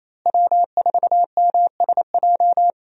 Morse code, heard as a keyed tone, about 27 wpm, 700 Hz. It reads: W4MSJ
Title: USA Call Signs - 5 Characters - Repeated 2x - 28wpm